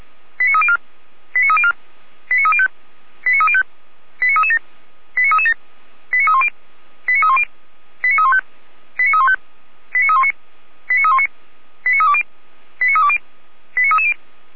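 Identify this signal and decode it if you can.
Five-tone selective call (ZVEI), heard as quick runs of five tones, tones of about 70 ms. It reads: 89304 89304 89305 89305 89337 89337 89310 89310 89315 89315 89319 89319 89320 89320 89330
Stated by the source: ZVEI five-tone selective call received on SDR.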